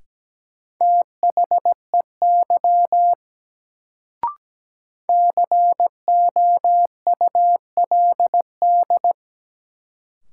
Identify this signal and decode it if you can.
Morse code, heard as a keyed tone, about 17 wpm, 700 Hz. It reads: THEY COULD
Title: Sentences from Top 100 Words - New Word Drill - 17wpm